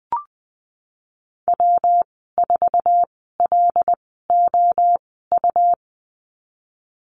Morse code, heard as a keyed tone, about 20 wpm, spaced as for 14 wpm, 700 Hz. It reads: W4LOU
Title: 186 Lesson 30g - Letter b - Call Signs 20wpm